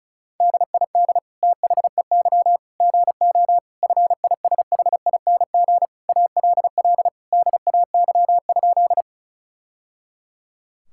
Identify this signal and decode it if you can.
Morse code, heard as a keyed tone, about 36 wpm, 700 Hz. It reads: DID THEY GO FISHING ALL DAY?
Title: Sentences from Top 1000 Words - Review Part 5 - 35wpm